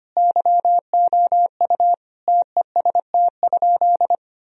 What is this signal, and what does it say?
Morse code, heard as a keyed tone, about 25 wpm, 700 Hz. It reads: YOU TEST?
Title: Sentences from Top 1000 Words - Review Part 3 - 25wpm